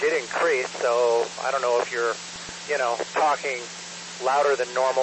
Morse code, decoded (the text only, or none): none